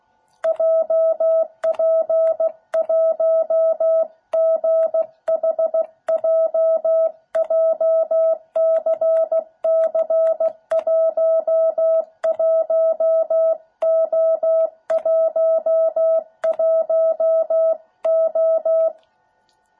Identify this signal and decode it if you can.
Morse code, heard as a keyed tone, about 16 wpm, 660 Hz. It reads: JP1GHJ JCC11O11O